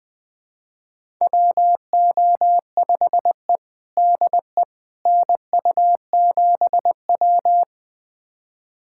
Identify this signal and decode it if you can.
Morse code, heard as a keyed tone, about 21 wpm, 700 Hz. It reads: WO5E DE NU7W
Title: USA Callsign DE Callsign - 4 Characters - 20wpm